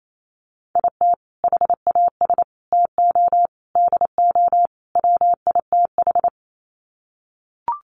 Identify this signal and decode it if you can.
Morse code, heard as a keyed tone, about 28 wpm, 700 Hz. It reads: IT HAS TO DO WITH
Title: Sets of 5 Words 28wpm